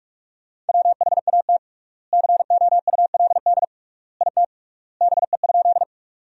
Morse code, heard as a keyed tone, about 43 wpm, 700 Hz. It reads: WHAT COULD IT BE?
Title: 3x Word-Spacing - Sentences from Top 100 Words - 45wpm